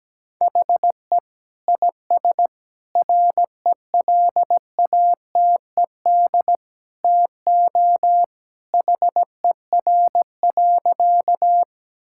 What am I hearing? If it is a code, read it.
Morse code, heard as a keyed tone, about 17 wpm, 700 Hz. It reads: HE IS RELATED TO HER.
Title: Sentences from Top 1000 Words - Review Part 5 - 17wpm